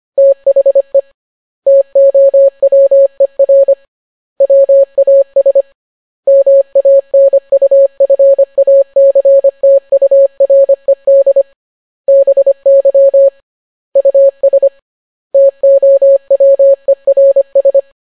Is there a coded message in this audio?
Yes: Morse code from 0.2 to 17.8 s, heard as a keyed tone, about 25 wpm, 550 Hz. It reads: THE TOWER WAS MANUFACTURED BY US TOWERS